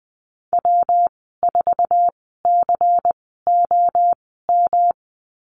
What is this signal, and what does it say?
Morse code, heard as a keyed tone, about 20 wpm, spaced as for 15 wpm, 700 Hz. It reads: W4COM
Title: 179 Lesson 29g - Slash - Call Signs 20wpm